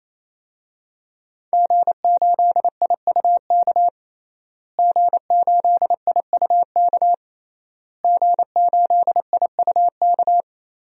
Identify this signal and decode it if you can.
Morse code, heard as a keyed tone, about 28 wpm, 700 Hz. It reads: G8IUK G8IUK G8IUK
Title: DXCC-100 Call Signs - 5 Characters - Repeated 3x - 28wpm